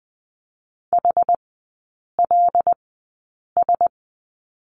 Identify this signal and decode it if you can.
Morse code, heard as a keyed tone, about 20 wpm, spaced as for 8 wpm, 700 Hz. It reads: HLS
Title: 093 Lesson 16e - Letter c - Three Characters 20wpm